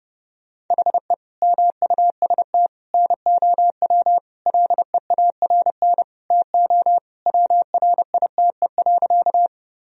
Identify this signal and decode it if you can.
Morse code, heard as a keyed tone, about 30 wpm, 700 Hz. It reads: HE MUST NOW LEARN TO WRITE.
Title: Sentences from Top 1000 Words - Review Part 1 - 30wpm